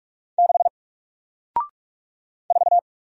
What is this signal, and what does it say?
Morse code, heard as a keyed tone, about 44 wpm, 700 Hz. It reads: 6 4
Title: Single Character (Letter:Number:Punctuation:Prosign) Alt 45wpm